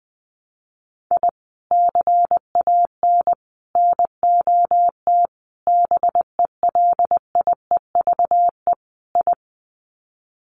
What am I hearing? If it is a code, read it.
Morse code, heard as a keyed tone, about 20 wpm, 700 Hz. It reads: I CAN NOT BELIEVE I
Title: Sets of 5 Words Encore2 20wpm